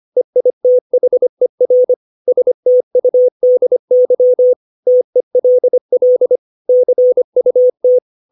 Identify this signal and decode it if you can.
Morse code, heard as a keyed tone, about 25 wpm, 500 Hz. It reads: EITHER STUDY TELL CUT
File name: 632 words / 632 Words 25WPM